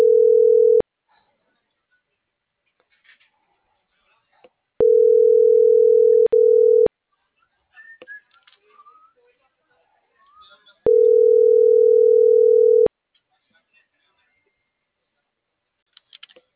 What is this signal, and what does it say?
Ringback tone (North America), 440 + 480 Hz.